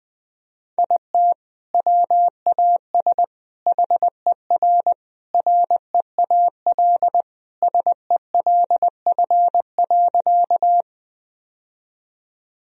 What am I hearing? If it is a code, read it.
Morse code, heard as a keyed tone, about 21 wpm, 700 Hz. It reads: IT WAS HER REAL SELF.
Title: Sentences from Top 1000 Words - Review Part 1 - 20wpm